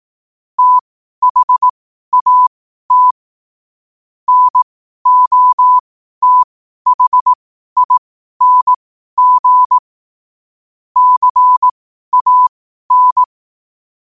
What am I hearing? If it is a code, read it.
Morse code, heard as a keyed tone, about 18 wpm, spaced as for 12 wpm, 1000 Hz. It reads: THAT NOTHING CAN
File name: qotdfiles / 12wpm